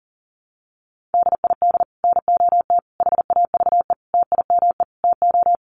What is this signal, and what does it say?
Morse code, heard as a keyed tone, about 37 wpm, 700 Hz. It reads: DID NOT HAVE TIME TO
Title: Sets of 5 Words Encore 40wpm